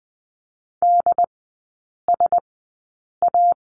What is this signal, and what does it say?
Morse code, heard as a keyed tone, about 20 wpm, spaced as for 8 wpm, 700 Hz. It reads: DSA
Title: 133 Lesson 22e - Question Mark - Three Characters 20wpm